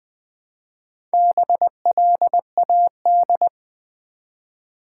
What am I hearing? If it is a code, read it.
Morse code, heard as a keyed tone, about 20 wpm, 700 Hz. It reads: BLAD
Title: Build A Word - 1 Letter At A Time - 5-Letter Words - 20wpm